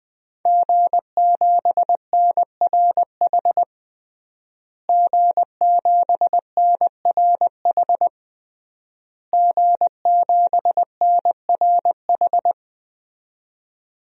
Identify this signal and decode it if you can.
Morse code, heard as a keyed tone, about 20 wpm, 700 Hz. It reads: G7NRH G7NRH G7NRH
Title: DXCC-100 Call Signs - 5 Characters - Repeated 3x - 20wpm